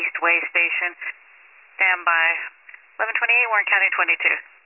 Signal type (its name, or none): Emergency Alert System (SAME) tones